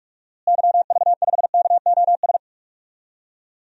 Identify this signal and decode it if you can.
Morse code, heard as a keyed tone, about 46 wpm, 700 Hz. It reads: YV5KOS
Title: DXCC-100 Call Signs - 6 Characters - Repeated 2x - 45wpm